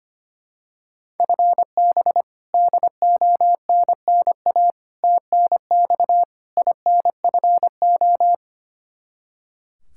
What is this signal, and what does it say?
Morse code, heard as a keyed tone, about 25 wpm, 700 Hz. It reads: FB DONNA TNX INFO
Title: Practice QSO - Exchange 3 - More Info - 25wpm